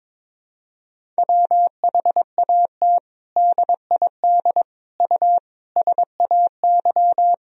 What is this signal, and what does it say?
Morse code, heard as a keyed tone, about 22 wpm, 700 Hz. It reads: WHAT DID U SAY
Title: Sentences from Top 200 Words and Abbreviations 22wpm